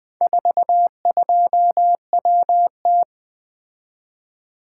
Morse code, heard as a keyed tone, about 24 wpm, 700 Hz. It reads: 42WT
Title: Speed-Racing - Four Letter-Number Combinations - 20wpm